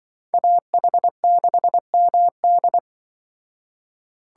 Morse code, heard as a keyed tone, about 23 wpm, 700 Hz. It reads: AH6MD
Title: Speed Racing - USA 2x2 Call Signs 20wpm